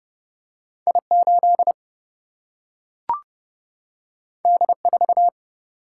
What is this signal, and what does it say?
Morse code, heard as a keyed tone, about 30 wpm, 700 Hz. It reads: I 8 D 4
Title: Single-Letter Single-Number 30wpm